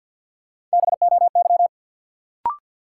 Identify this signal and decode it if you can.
Morse code, heard as a keyed tone, about 52 wpm, 700 Hz. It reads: BOY
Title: Sentences from Top 200 Words - New Word Drill - 50wpm